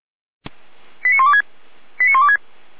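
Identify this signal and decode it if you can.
Five-tone selective call (ZVEI), heard as quick runs of five tones, tones of about 70 ms. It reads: 89126 89126
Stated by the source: Five-tone ZVEI selective call 89092, received on an SDR.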